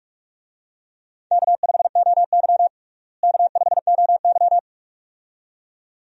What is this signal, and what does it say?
Morse code, heard as a keyed tone, about 46 wpm, 700 Hz. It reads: K5OY K5OY
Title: USA Call Signs - 4 Characters - Repeated 2x - 45wpm